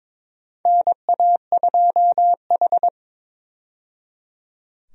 Morse code, heard as a keyed tone, about 22 wpm, 700 Hz. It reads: NA2H
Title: USA Call Signs - 4 Characters - 22wpm